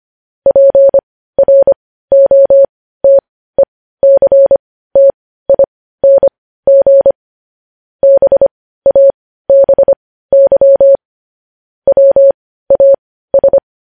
Morse code, heard as a keyed tone, about 25 wpm, spaced as for 15 wpm, 550 Hz. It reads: PROTECTING BABY WAS